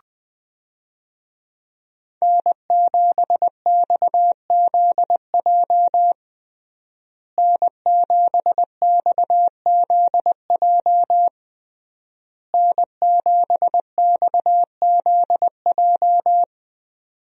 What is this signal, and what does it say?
Morse code, heard as a keyed tone, about 20 wpm, 700 Hz. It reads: N7XZJ N7XZJ N7XZJ
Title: USA Call Signs - 5 Characters - Repeated 3x - 20wpm